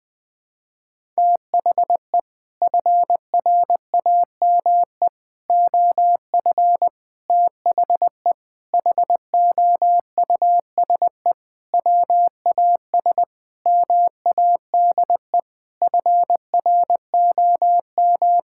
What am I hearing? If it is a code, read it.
Morse code, heard as a keyed tone, about 21 wpm, 700 Hz. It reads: THE FRAME OF THE HOUSE WAS MADE FROM
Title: Sentences from Top 1000 Words - Review Part 6 - 20wpm